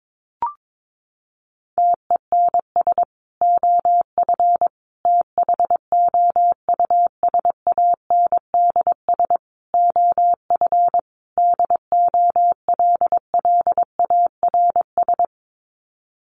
Morse code, heard as a keyed tone, about 22 wpm, 700 Hz. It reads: TENS OF THOUSANDS OF DOLLARS
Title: Sets of 5 Words Encore2 22wpm